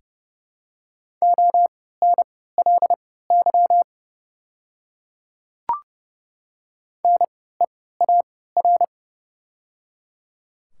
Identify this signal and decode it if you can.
Morse code, heard as a keyed tone, about 29 wpm, spaced as for 17 wpm, 700 Hz. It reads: ONLY NEAR
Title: 3x Character Spacing - Top 200 Words - 30wpm